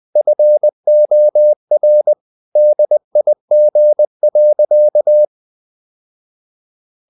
Morse code, heard as a keyed tone, about 20 wpm, 600 Hz. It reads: FOR DIG.
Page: Ferie 20 wpm